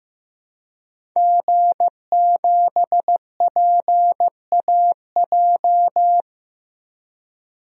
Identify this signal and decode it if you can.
Morse code, heard as a keyed tone, about 16 wpm, 700 Hz. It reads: G7PAJ
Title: DXCC-100 Call Signs - 5 Characters - 15wpm